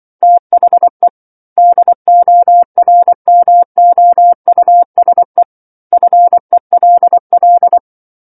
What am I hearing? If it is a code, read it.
Morse code, heard as a keyed tone, about 24 wpm, 700 Hz. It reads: THE DORMOUSE FELL
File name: 07-Chapter-07 24 WPM